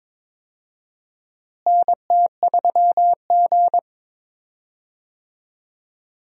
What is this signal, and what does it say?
Morse code, heard as a keyed tone, about 22 wpm, 700 Hz. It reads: NT3G
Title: USA Call Signs - 4 Characters - Repeated 2x - 22wpm